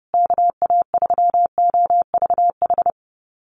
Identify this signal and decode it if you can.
Morse code, heard as a keyed tone, about 30 wpm, 700 Hz. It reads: KA3OVH
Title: USA Call Signs Repeated 2x 30wpm